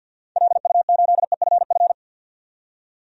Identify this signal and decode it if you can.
Morse code, heard as a keyed tone, about 53 wpm, 700 Hz. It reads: LU8EFF
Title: DXCC-100 Call Signs - 6 Characters - 50wpm